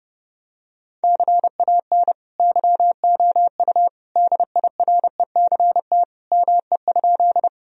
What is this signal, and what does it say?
Morse code, heard as a keyed tone, about 30 wpm, 700 Hz. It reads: CAN YOU DIRECT ME?
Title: Sentences from Top 1000 Words - Review Part 2 - 30wpm